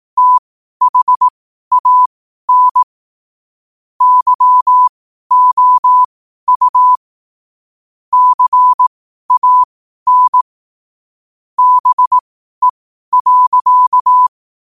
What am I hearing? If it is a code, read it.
Morse code, heard as a keyed tone, about 18 wpm, spaced as for 12 wpm, 1000 Hz. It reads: THAN YOU CAN BE.
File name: qotdfiles / 12wpm